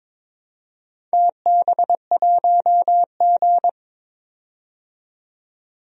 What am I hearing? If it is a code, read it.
Morse code, heard as a keyed tone, about 22 wpm, 700 Hz. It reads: TB1G
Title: DXCC-100 Call Signs - 4 Characters - 22wpm